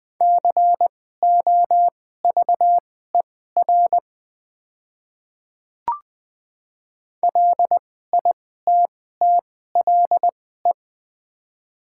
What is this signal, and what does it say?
Morse code, heard as a keyed tone, about 20 wpm, spaced as for 14 wpm, 700 Hz. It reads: COVER LITTLE
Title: 2x Character Spacing - Top 500 Words - 20wpm